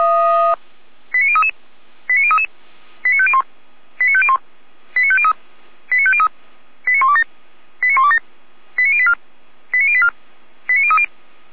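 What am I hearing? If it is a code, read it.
Five-tone selective call (ZVEI), heard as quick runs of five tones, tones of about 70 ms. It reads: 89033 89033 89501 89501 89503 89503 89127 89127 89084 89084 89030
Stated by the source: ZVEI five-tone call 89085 received by SDR.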